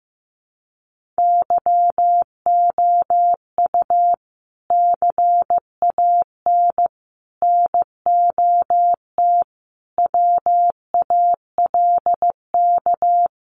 Morse code, heard as a keyed tone, about 14 wpm, 700 Hz. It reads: YOU CAN NOT WALK
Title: Sentences from Top 1500 Words 15wpm